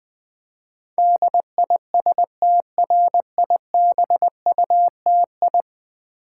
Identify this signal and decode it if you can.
Morse code, heard as a keyed tone, about 20 wpm, 700 Hz. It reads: DISTRIBUTI
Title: Build A Word - 5 Letters At A Time - 12-Letter Words - 20wpm